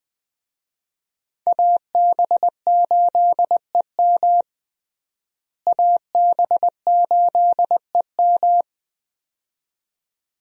Morse code, heard as a keyed tone, about 20 wpm, 700 Hz. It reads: AB8EM AB8EM
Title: USA Call Signs - 5 Characters - Repeated 2x - 20wpm